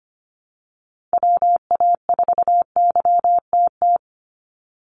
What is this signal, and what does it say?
Morse code, heard as a keyed tone, about 25 wpm, 700 Hz. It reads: WA4YTT
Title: USA Call Signs 25wpm